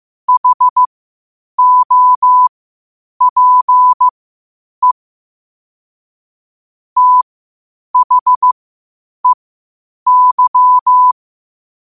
Morse code, heard as a keyed tone, about 15 wpm, spaced as for 8 wpm, 1000 Hz. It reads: HOPE THEY
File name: 8wpm